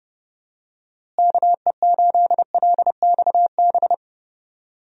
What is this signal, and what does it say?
Morse code, heard as a keyed tone, about 32 wpm, 700 Hz. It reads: KE8LXB
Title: USA Call Signs - 6 Characters - Repeated 2x - 30wpm